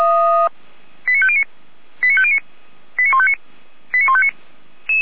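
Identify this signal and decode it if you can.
Five-tone selective call (ZVEI), heard as quick runs of five tones, tones of about 70 ms. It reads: 89509 89509 89159 89159
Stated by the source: ZVEI five-tone selective call, received by SDR, archived by feuerwehr-orlamuende.org (fire brigade).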